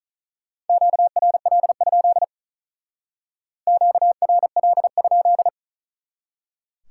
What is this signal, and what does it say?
Morse code, heard as a keyed tone, about 45 wpm, 700 Hz. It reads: QRL? QRL?